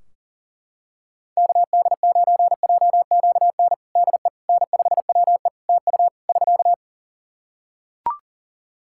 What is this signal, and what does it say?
Morse code, heard as a keyed tone, about 40 wpm, 700 Hz. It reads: KD9JQN DE N5WE TU <SK>
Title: Practice QSO - Exchange 4 - Ending - 40wpm